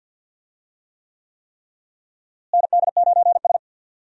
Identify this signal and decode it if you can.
Morse code, heard as a keyed tone, about 52 wpm, 700 Hz. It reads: ND9S
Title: USA Call Signs - 4 Characters - Repeated 3x - 50wpm